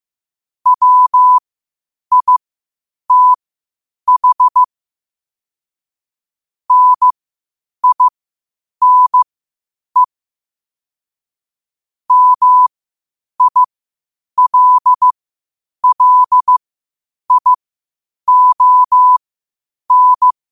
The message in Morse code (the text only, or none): WITH NINE MILLION